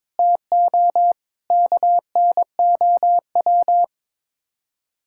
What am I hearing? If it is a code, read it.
Morse code, heard as a keyed tone, about 22 wpm, 700 Hz. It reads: TO KNOW
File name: practice / 22wpm / Sets of 7 Words - 22wpm